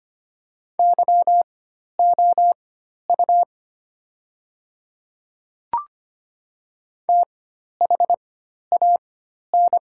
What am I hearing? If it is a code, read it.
Morse code, heard as a keyed tone, about 25 wpm, spaced as for 12 wpm, 700 Hz. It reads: YOU THAN
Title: 4x Character Spacing - Top 100 Words - 25wpm